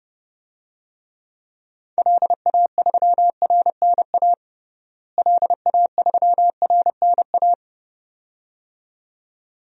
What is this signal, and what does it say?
Morse code, heard as a keyed tone, about 30 wpm, 700 Hz. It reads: LA3RNA LA3RNA
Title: DXCC-100 Call Signs - 6 Characters - Repeated 2x - 30wpm